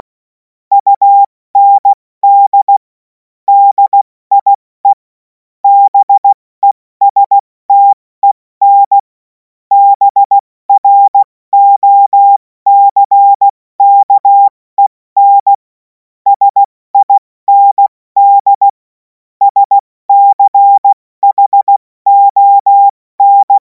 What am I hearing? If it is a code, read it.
Morse code, heard as a keyed tone, about 16 wpm, spaced as for 14 wpm, 800 Hz. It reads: UND DIE BESTEN BROCKEN SIND SCHON
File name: Morsekurs / Witze / Witz 17. März 2026 70 F80